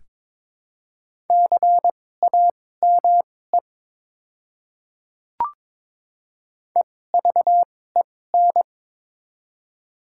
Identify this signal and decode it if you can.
Morse code, heard as a keyed tone, about 22 wpm, spaced as for 16 wpm, 690 Hz. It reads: CAME EVEN
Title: 2x Character Spacing - Top 200 Words - 22wpm